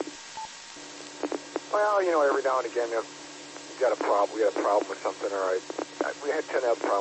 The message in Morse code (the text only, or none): none